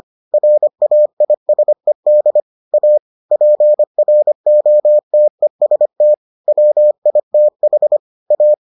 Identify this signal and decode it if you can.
Morse code, heard as a keyed tone, about 25 wpm, 600 Hz. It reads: RAISED A PROTEST WITH A